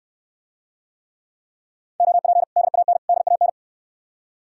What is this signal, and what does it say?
Morse code, heard as a keyed tone, about 69 wpm, 700 Hz. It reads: OP DAN DAN